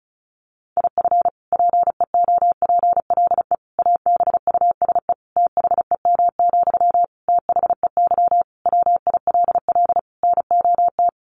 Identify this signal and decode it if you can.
Morse code, heard as a keyed tone, about 34 wpm, 700 Hz. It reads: IF PEOPLE ABUSE THEM, THEY WILL NOT
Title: Sentences from Top 1300 Words 35wpm